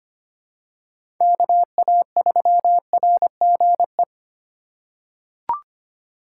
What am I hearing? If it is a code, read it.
Morse code, heard as a keyed tone, about 25 wpm, 700 Hz. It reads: KA3RGE
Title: USA Call Signs - 6 Characters - 2x Character Spacing - 25wpm